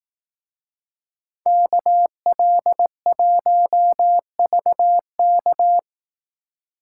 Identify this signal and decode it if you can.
Morse code, heard as a keyed tone, about 18 wpm, 700 Hz. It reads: KL1VK
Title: Speed Racing - USA 2x2 Call Signs 15wpm